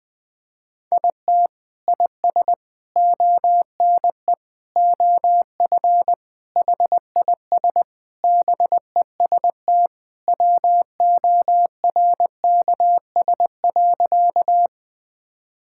Morse code, heard as a keyed tone, about 20 wpm, 700 Hz. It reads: IT IS ONE OF HIS BEST WORKS.
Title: Sentences with 3-Word Phrases - Spoken Beginning - 20wpm